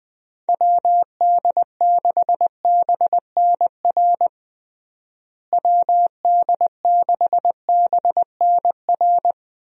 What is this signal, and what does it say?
Morse code, heard as a keyed tone, about 21 wpm, 700 Hz. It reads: WD6BNR WD6BNR
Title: USA Call Signs - 6 Characters - Repeated 3x - 20wpm